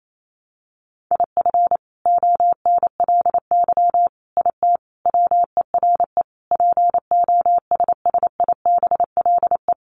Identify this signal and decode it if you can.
Morse code, heard as a keyed tone, about 28 wpm, 700 Hz. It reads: IF ONLY IT WERE POSSIBLE
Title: Sentences from Top 500 Words 28wpm